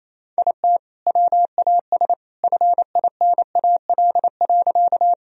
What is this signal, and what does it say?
Morse code, heard as a keyed tone, about 28 wpm, 700 Hz. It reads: IT WAS FINAL.
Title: Sentences from Top 1000 Words - Review Part 2 - 28wpm